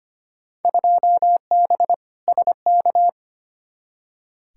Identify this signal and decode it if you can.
Morse code, heard as a keyed tone, about 25 wpm, 700 Hz. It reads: 2B SK
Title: Field Day Exchanges Encore - 25wpm